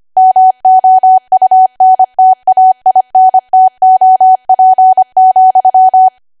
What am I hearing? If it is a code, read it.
Morse code, heard as a keyed tone, about 25 wpm, 750 Hz. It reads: MOUNTAINTOP,
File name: Archive / 25 WPM / 25WPM